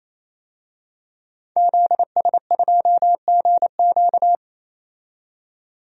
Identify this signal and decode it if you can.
Morse code, heard as a keyed tone, about 28 wpm, 700 Hz. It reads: ZS2GQ
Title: DXCC-100 Call Signs - 5 Characters - 28wpm